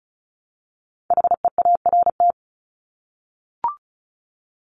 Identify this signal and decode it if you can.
Morse code, heard as a keyed tone, about 34 wpm, 700 Hz. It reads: HEART
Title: Top 1001-1250 Words 35wpm